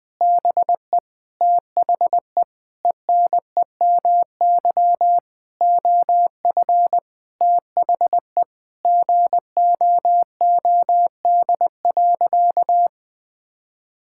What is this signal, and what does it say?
Morse code, heard as a keyed tone, about 21 wpm, 700 Hz. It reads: BE THE ENEMY OF THE GOOD.
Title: Sentences with 9-Word Phrases - 20wpm